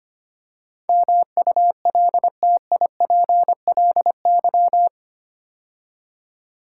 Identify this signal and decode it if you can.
Morse code, heard as a keyed tone, about 25 wpm, 690 Hz. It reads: MULTIPLY
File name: 8 Letter Words - First 3 Letters Given - 25wpm